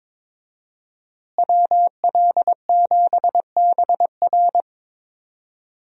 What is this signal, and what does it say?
Morse code, heard as a keyed tone, about 22 wpm, 700 Hz. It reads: WL7BR
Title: USA Call Signs - 5 Characters - Repeated 3x - 22wpm